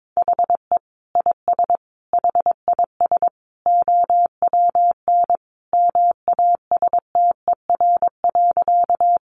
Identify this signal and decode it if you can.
Morse code, heard as a keyed tone, about 22 wpm, 700 Hz. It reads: HE IS HIS OWN MASTER.
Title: Sentences from Top 1300 Words 22wpm